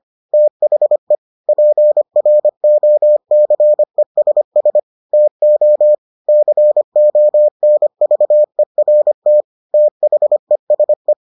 Morse code, heard as a keyed tone, about 25 wpm, 600 Hz. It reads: THE PROCESS TO CONVERT THESE